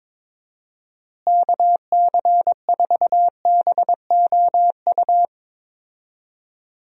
Morse code, heard as a keyed tone, about 22 wpm, 700 Hz. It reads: KC4BOU
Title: USA Call Signs - 6 Characters - Repeated 2x - 22wpm